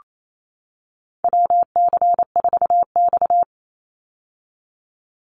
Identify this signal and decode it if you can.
Morse code, heard as a keyed tone, about 28 wpm, 700 Hz. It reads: WC4X
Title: Call Signs - 28wpm